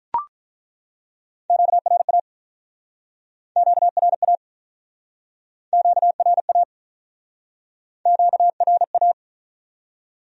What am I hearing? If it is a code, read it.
Morse code, heard as a keyed tone, about 50 wpm, 700 Hz. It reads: QRA QRA QRA QRA